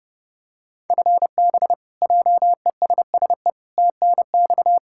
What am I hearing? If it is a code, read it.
Morse code, heard as a keyed tone, about 30 wpm, 700 Hz. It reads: FB JESSE TNX
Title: K1USN SST QSO Practice 30wpm